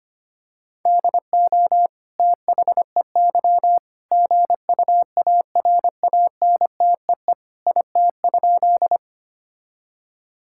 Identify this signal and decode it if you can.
Morse code, heard as a keyed tone, about 25 wpm, 700 Hz. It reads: DO THEY GUARANTEE IT?
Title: Sentences from Top 1600 Words 25wpm